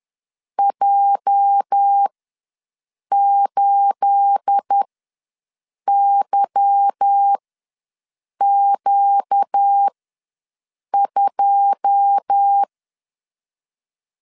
Morse code, heard as a keyed tone, about 10 wpm, spaced as for 6 wpm, 800 Hz. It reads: J8YQ2
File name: LEZIONI / 10 SIGNS / 10A SLASH ws 10wpm 4121